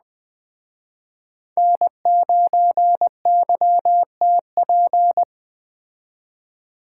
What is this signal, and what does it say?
Morse code, heard as a keyed tone, about 20 wpm, 700 Hz. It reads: N9YTP
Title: Speed Racing - USA 1x3 Call Signs 17wpm